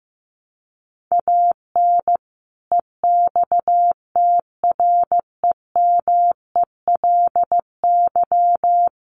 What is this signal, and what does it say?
Morse code, heard as a keyed tone, about 15 wpm, 700 Hz. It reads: AN EXTREMELY
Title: Sentences from Top 1200 Words 15wpm